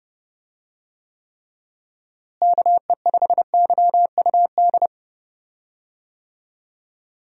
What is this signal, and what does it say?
Morse code, heard as a keyed tone, about 32 wpm, 700 Hz. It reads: KE5YUD
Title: USA Call Signs - 6 Characters - Repeated 2x - 30wpm